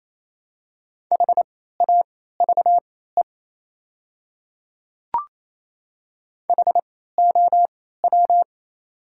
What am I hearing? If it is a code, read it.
Morse code, heard as a keyed tone, about 28 wpm, spaced as for 16 wpm, 700 Hz. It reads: HAVE HOW
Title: 3x Character Spacing - Top 100 Words - 28wpm